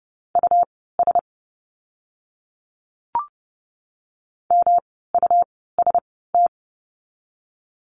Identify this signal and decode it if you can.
Morse code, heard as a keyed tone, about 30 wpm, spaced as for 17 wpm, 700 Hz. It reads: US MUST
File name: course / 30:10wpm / 237 Lesson 38f - Letter q - Words 30wpm